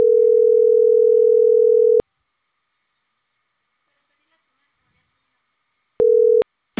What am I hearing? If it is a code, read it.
Ringback tone (North America), 440 + 480 Hz.